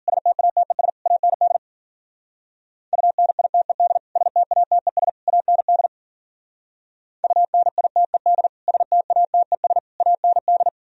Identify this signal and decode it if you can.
Morse code, heard as a keyed tone, about 52 wpm, 700 Hz. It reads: STATES AND UNITED STATES AND UNITED STATES AND